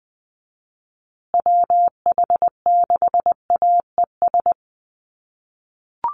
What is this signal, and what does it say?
Morse code, heard as a keyed tone, about 20 wpm, 700 Hz. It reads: WH6AES
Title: USA Call Signs Repeated 3x 20wpm